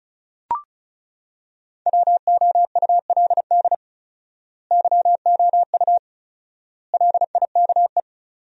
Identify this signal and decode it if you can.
Morse code, heard as a keyed tone, about 36 wpm, 700 Hz. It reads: WOULD YOU LIKE
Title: 4x Word-Spacing - Sentences from Top 100 Words - 35wpm